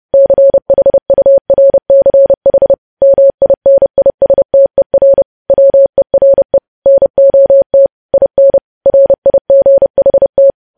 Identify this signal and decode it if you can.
Morse code, heard as a keyed tone, about 30 wpm, 550 Hz. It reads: CHURCH MINISTER WERE NOT IN RIGHT
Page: news headlines in morse code at 30 wpm